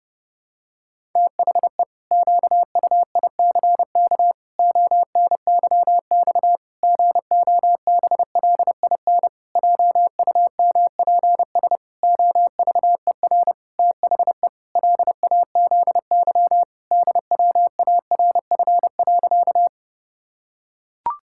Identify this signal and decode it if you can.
Morse code, heard as a keyed tone, about 32 wpm, 700 Hz. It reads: THE QUICK ONYX GOBLIN JUMPS OVER THE LAZY DWARF.
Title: Pangram Challenge v2 - 30wpm